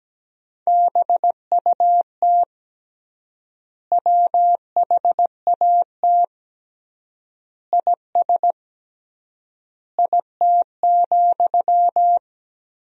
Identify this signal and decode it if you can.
Morse code, heard as a keyed tone, about 17 wpm, 700 Hz. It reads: BUT WHAT IS IT,
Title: 3x Word-Spacing - Sentences from Top 300 Words - 17wpm